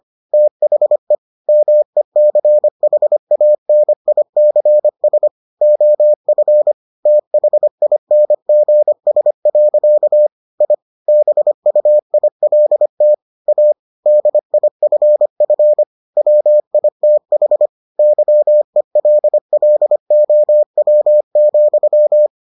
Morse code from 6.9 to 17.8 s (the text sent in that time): THINGS. I BUILT A DIFF WITH